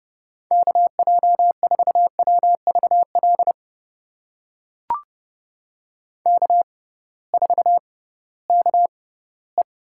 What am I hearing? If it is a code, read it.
Morse code, heard as a keyed tone, about 29 wpm, spaced as for 17 wpm, 700 Hz. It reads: KJ4WVL K4KE
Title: USA Call Signs - 6x Character Spacing - 30wpm